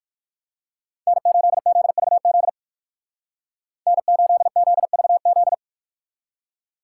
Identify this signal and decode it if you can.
Morse code, heard as a keyed tone, about 51 wpm, 700 Hz. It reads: N8ZVZ N8ZVZ
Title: Speed Racing - USA 1x3 Call Signs 45wpm